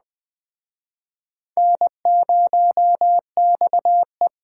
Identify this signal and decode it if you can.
Morse code, heard as a keyed tone, about 21 wpm, 700 Hz. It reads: N0XE